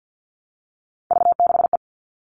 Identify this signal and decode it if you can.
Morse code, heard as a keyed tone, about 50 wpm, 700 Hz. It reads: V6E
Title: All Cut Numbers - 3-Digit 50wpm